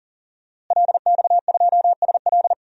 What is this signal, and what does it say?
Morse code, heard as a keyed tone, about 40 wpm, 700 Hz. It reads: LX2SL
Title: DXCC-100 Call Signs - 5 Characters - Repeated 3x - 40wpm